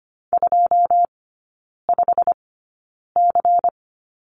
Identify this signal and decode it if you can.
Morse code, heard as a keyed tone, about 25 wpm, spaced as for 9 wpm, 700 Hz. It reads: 25C
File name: course / 25:10wpm / 093 Lesson 16e - Letter c - Three Characters 25wpm